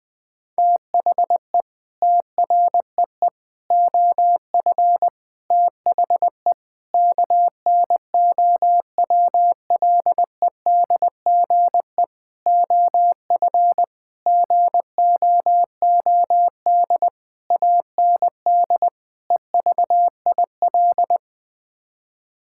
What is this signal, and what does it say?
Morse code, heard as a keyed tone, about 21 wpm, 700 Hz. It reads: THE TREE OF THE KNOWLEDGE OF GOOD AND EVIL
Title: Sets of 9 Words - 20wpm